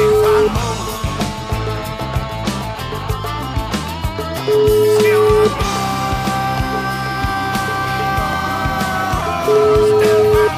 Ringback tone (Europe, 425 Hz), with noise.